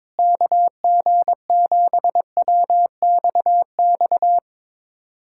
Morse code, heard as a keyed tone, about 21 wpm, 700 Hz. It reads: KG7WXX
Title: USA Callsign DE Callsign 22wpm